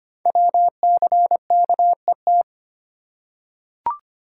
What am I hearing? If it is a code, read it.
Morse code, heard as a keyed tone, about 25 wpm, 700 Hz. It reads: WCKET
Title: Code Groups - 5 Letters - 25wpm